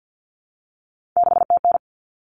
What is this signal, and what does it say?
Morse code, heard as a keyed tone, about 50 wpm, 700 Hz. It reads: 6TN
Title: Common Cut Numbers - 3-Digit 50wpm